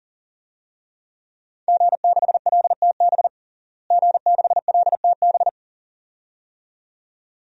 Morse code, heard as a keyed tone, about 40 wpm, 700 Hz. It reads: G6LTB G6LTB
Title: DXCC-100 Call Signs - 5 Characters - Repeated 2x - 40wpm